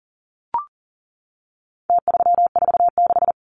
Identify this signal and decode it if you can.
Morse code, heard as a keyed tone, about 40 wpm, 700 Hz. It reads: T346
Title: Numbers - Contest Style - 4-Digit 40wpm